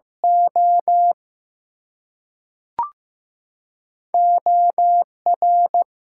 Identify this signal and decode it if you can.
Morse code, heard as a keyed tone, about 15 wpm, spaced as for 3 wpm, 700 Hz. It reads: OO R
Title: Two Letter Words - 15wpm